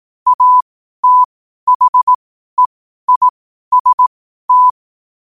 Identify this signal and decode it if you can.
Morse code, heard as a keyed tone, about 18 wpm, spaced as for 12 wpm, 1000 Hz. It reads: ATHEIST
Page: Quotes for Fri, 15 Aug 2025 in Morse Code at 12 words per minute.